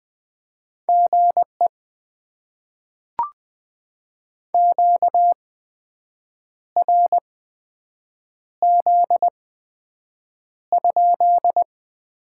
Morse code, heard as a keyed tone, about 20 wpm, spaced as for 7 wpm, 700 Hz. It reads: GE QRZ?
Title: Top 90 QSO Elements - 8x Character Spacing - 20wpm